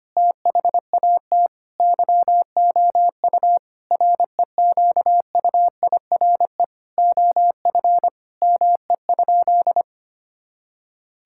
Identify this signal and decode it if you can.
Morse code, heard as a keyed tone, about 25 wpm, 700 Hz. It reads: THAT YOU REQUIRE OF ME?
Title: Sentences from Top 1000 Words - Review Part 6 - 25wpm